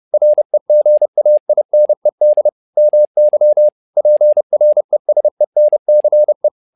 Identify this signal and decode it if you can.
Morse code, heard as a keyed tone, about 30 wpm, 600 Hz. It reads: REGAINED MY PRESENCE
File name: learning center / tower trouble / tower trouble 30wpm